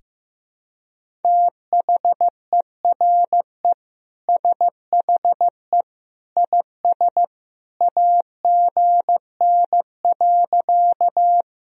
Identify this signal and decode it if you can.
Morse code, heard as a keyed tone, about 15 wpm, 700 Hz. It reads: THERE SHE IS AGN.
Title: Sentences from Top 200 Words and Abbreviations 15wpm